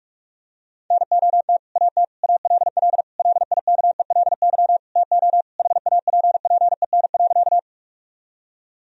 Morse code, heard as a keyed tone, about 46 wpm, 700 Hz. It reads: NOT AT ALL LIKELY TO HAPPEN.
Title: Sentences from Top 1000 Words - Review Part 4 - 45wpm